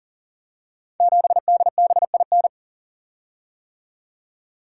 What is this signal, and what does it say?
Morse code, heard as a keyed tone, about 43 wpm, 700 Hz. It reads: 7DBIN
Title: Code Groups - 5 Letter:Numbers - 40wpm